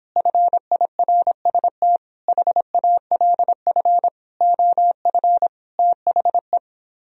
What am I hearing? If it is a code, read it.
Morse code, heard as a keyed tone, about 26 wpm, 700 Hz. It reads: FIRST HALF OF THE